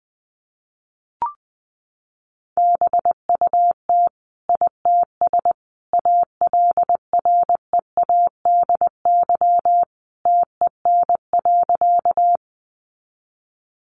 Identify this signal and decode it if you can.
Morse code, heard as a keyed tone, about 20 wpm, 700 Hz. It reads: BUT ITS ALREADY TEN.
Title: Sentences from Top 700 Words 20wpm